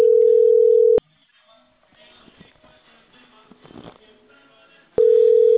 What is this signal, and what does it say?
Ringback tone (North America), 440 + 480 Hz.